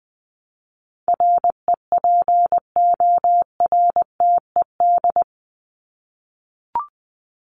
Morse code, heard as a keyed tone, about 20 wpm, 700 Hz. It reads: REPORTED
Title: ED Suffix - Top 100 Words with Base Words 20wpm